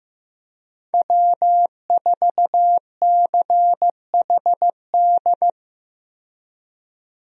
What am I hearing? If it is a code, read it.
Morse code, heard as a keyed tone, about 15 wpm, 700 Hz. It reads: W4CHD
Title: USA Call Signs - 5 Characters - 15wpm